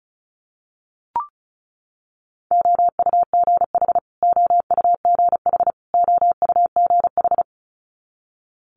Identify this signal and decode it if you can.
Morse code, heard as a keyed tone, about 36 wpm, 700 Hz. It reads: OUGH OUGH OUGH
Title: Multi-letter Phonemes Repeated 3x 35wpm